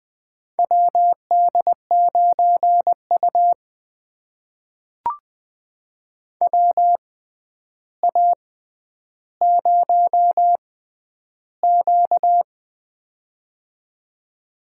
Morse code, heard as a keyed tone, about 20 wpm, spaced as for 10 wpm, 700 Hz. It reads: WD9U WA0Q
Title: USA Call Signs - 4 Characters - 6x Character Spacing - 20wpm